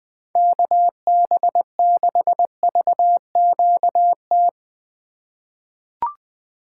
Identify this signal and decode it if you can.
Morse code, heard as a keyed tone, about 21 wpm, 700 Hz. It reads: KB6VQT
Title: USA Call Signs - 6 Characters - 3x Character Spacing - 20wpm